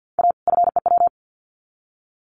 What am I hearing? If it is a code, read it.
Morse code, heard as a keyed tone, about 50 wpm, 700 Hz. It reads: A FEW
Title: Sets of 2 Words 50wpm